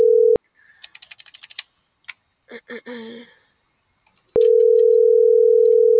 Ringback tone (North America), 440 + 480 Hz.